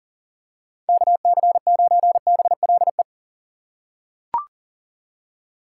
Morse code, heard as a keyed tone, about 40 wpm, 700 Hz. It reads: KC9BLE
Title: USA Call Signs - 6 Characters - 4x Character Spacing - 40wpm